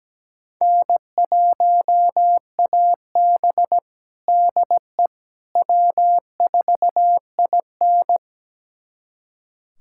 Morse code, heard as a keyed tone, about 17 wpm, 700 Hz. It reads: N1AB DE W4IN
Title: USA Callsign DE Callsign - 4 Characters - 17wpm